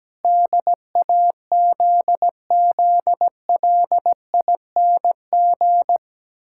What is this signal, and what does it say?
Morse code, heard as a keyed tone, about 17 wpm, 700 Hz. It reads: DAZZLING
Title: Pangram Challenge v2 - 17wpm